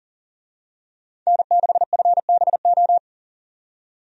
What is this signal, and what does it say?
Morse code, heard as a keyed tone, about 40 wpm, 700 Hz. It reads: N6FBO
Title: USA Call Signs - 5 Characters - 40wpm